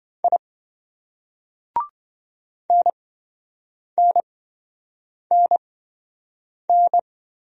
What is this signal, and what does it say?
Morse code, heard as a keyed tone, about 28 wpm, spaced as for 7 wpm, 700 Hz. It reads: I NNNN